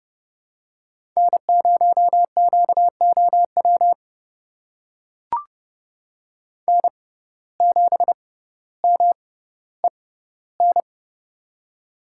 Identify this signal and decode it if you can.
Morse code, heard as a keyed tone, about 29 wpm, spaced as for 15 wpm, 700 Hz. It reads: N0QOW N7MEN
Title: USA Call Signs - 5 Characters - 6x Character Spacing - 30wpm